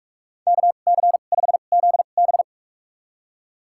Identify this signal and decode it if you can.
Morse code, heard as a keyed tone, about 47 wpm, spaced as for 33 wpm, 700 Hz. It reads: KC5ZB
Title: USA Call Signs - 5 Characters - 2x Character Spacing - 45wpm